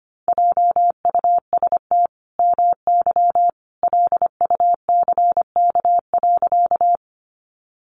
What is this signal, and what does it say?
Morse code, heard as a keyed tone, about 25 wpm, 700 Hz. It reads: JUST MY LUCK.
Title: Sentences from Top 1500 Words 25wpm